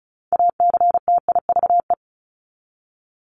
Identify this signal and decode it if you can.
Morse code, heard as a keyed tone, about 34 wpm, 700 Hz. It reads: ACTIVE
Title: Top 751-1000 Words 35wpm